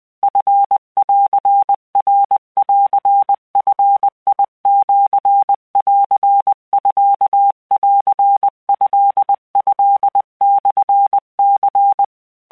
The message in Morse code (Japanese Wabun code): チンナンヂシンミントトモニ